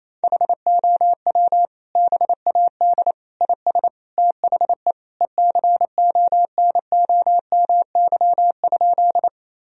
Morse code, heard as a keyed tone, about 27 wpm, 700 Hz. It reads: HOW BAD IS THE ECONOMY?
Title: Sentences from Top 1000 Words - Review Part 3 - 28wpm